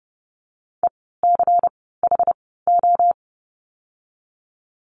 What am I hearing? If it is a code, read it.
Morse code, heard as a keyed tone, about 30 wpm, 700 Hz. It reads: E C H O
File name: Lesson 16f - Letter c - Words 30wpm